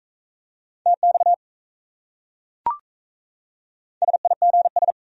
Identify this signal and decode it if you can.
Morse code, heard as a keyed tone, about 35 wpm, 700 Hz. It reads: TX SIGS
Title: Speed-Racing Abbreviations - 28wpm